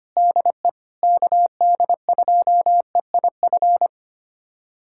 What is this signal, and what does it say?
Morse code, heard as a keyed tone, about 25 wpm, 700 Hz. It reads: DE KD2EIF
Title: USA Callsign DE Callsign - 6 Characters - 25wpm